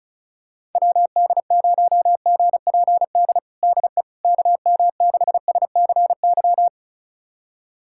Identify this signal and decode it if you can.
Morse code, heard as a keyed tone, about 36 wpm, 690 Hz. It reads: WD0GPD DE KM6SCY
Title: USA Callsign DE Callsign 35wpm